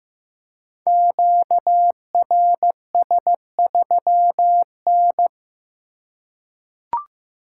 Morse code, heard as a keyed tone, about 15 wpm, 700 Hz. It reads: QRS3N
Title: Code Groups - 5 Letter:Numbers - 15wpm